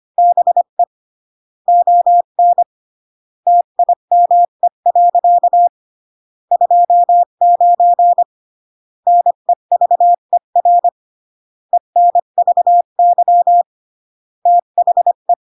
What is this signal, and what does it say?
Morse code, heard as a keyed tone, about 25 wpm, 700 Hz. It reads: BE ON TIME. 29 NEVER ENVY THE
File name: CW / jodorowsky / 82 commandments 25-20